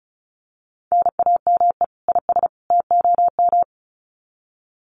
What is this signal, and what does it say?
Morse code, heard as a keyed tone, about 34 wpm, 700 Hz. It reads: NAME IS TOM
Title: Common QSO Phrases 35wpm